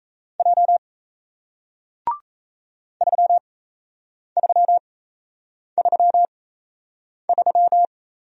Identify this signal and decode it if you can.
Morse code, heard as a keyed tone, about 38 wpm, spaced as for 8 wpm, 700 Hz. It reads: J 3333